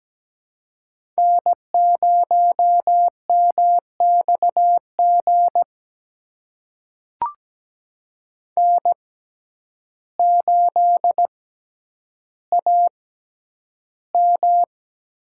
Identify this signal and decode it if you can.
Morse code, heard as a keyed tone, about 17 wpm, spaced as for 9 wpm, 700 Hz. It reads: N0MXG N8AM
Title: USA Call Signs - 5 Characters - 6x Character Spacing - 17wpm